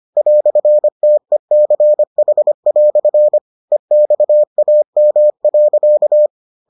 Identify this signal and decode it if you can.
Morse code, heard as a keyed tone, about 25 wpm, 600 Hz. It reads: "TECH" EXAM.